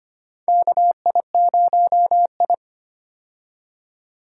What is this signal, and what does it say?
Morse code, heard as a keyed tone, about 25 wpm, 700 Hz. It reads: KI0I
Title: K1USN SST QSO Practice - 25wpm